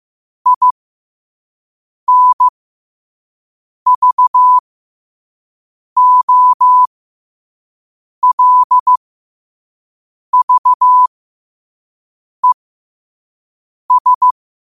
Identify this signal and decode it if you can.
Morse code, heard as a keyed tone, about 15 wpm, spaced as for 5 wpm, 1000 Hz. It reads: INVOLVES
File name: qotdfiles / 5wpm